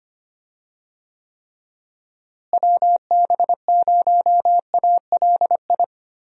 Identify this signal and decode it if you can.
Morse code, heard as a keyed tone, about 25 wpm, 700 Hz. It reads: WB0ALI
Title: USA Call Signs - 6 Characters - Repeated 2x - 25wpm